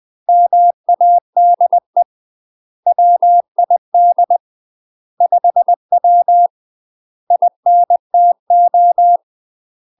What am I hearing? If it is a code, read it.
Morse code, heard as a keyed tone, about 20 wpm, 700 Hz. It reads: MADE WID 5W INTO
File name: UK MASTER QSOs TRAINING / 20WPM / UK MASTER QSOs 0064